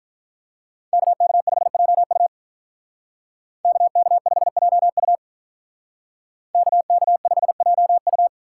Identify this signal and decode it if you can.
Morse code, heard as a keyed tone, about 46 wpm, 700 Hz. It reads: KK5JU KK5JU KK5JU